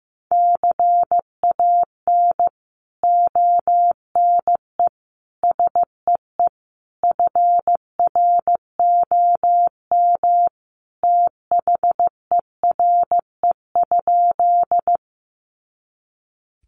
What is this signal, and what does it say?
Morse code, heard as a keyed tone, about 15 wpm, 700 Hz. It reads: CAN ONE SEE FROM THERE?
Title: Sentences from Top 100 Words 15wpm